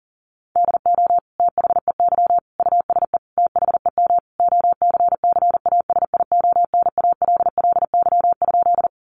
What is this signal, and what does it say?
Morse code, heard as a keyed tone, about 40 wpm, 700 Hz. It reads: DO THEY USE THEM OCCASIONALLY?
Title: Sentences from Top 1300 Words 40wpm